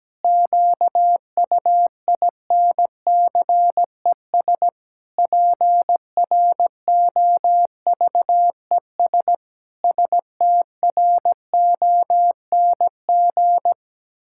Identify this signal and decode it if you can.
Morse code, heard as a keyed tone, about 17 wpm, 700 Hz. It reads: QUINCES PROVES STRONG
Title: Pangram Challenge v2 - 17wpm